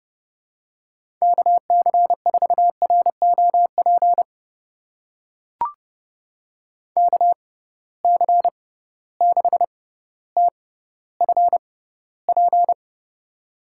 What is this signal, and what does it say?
Morse code, heard as a keyed tone, about 30 wpm, spaced as for 15 wpm, 700 Hz. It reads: KC4ROP KC6TFP
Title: USA Call Signs - 6 Characters - 6x Character Spacing - 30wpm